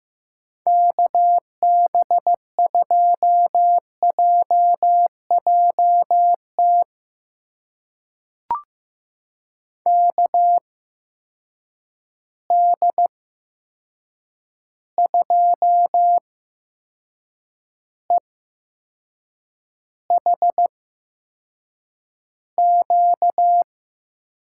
Morse code, heard as a keyed tone, about 15 wpm, spaced as for 6 wpm, 700 Hz. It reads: KB2JJT KD2EHQ
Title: USA Call Signs - 6 Characters - 8x Character Spacing - 15wpm